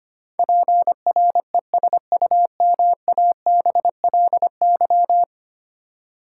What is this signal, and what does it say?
Morse code, heard as a keyed tone, about 25 wpm, 700 Hz. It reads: PRESUMABLY
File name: Ten Letter Words - 25wpm